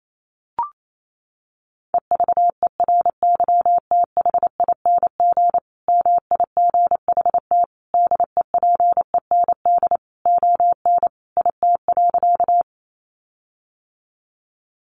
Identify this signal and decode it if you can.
Morse code, heard as a keyed tone, about 28 wpm, 700 Hz. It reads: EVERYTHING MIGHT DEPEND ON IT.
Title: Sentences from Top 1300 Words 28wpm